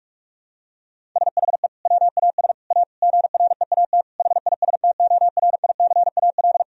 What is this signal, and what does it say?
Morse code, heard as a keyed tone, about 46 wpm, 700 Hz. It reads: SHE WAS A GREAT HISTORICAL
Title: Sentences from Top 1000 Words - Review Part 4 - 45wpm